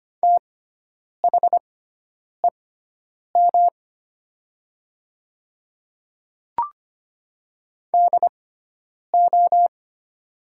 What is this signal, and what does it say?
Morse code, heard as a keyed tone, about 25 wpm, spaced as for 9 wpm, 700 Hz. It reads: THEM DO
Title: 6x Character Spacing - Top 100 Words - 25wpm